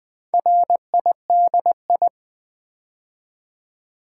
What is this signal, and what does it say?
Morse code, heard as a keyed tone, about 20 wpm, 700 Hz. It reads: RIDI
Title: Build A Word - 4 Letters At A Time - 10-Letter Words - 20wpm